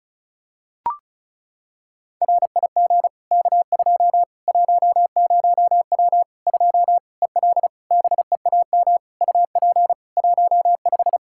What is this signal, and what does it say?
Morse code, heard as a keyed tone, about 36 wpm, 700 Hz. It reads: RIG K2 10W 2 EL BEAM UP 15
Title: Practice QSO - Exchange 2 - Info - 35wpm